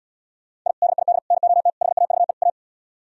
Morse code, heard as a keyed tone, about 77 wpm, 700 Hz. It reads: I DID NOT HAVE A